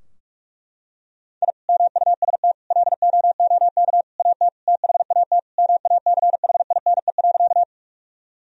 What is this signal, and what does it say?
Morse code, heard as a keyed tone, about 41 wpm, 700 Hz. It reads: I MUST LOOK AT THAT MACHINE.
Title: Sentences from Top 1000 Words - Review Part 2 - 45wpm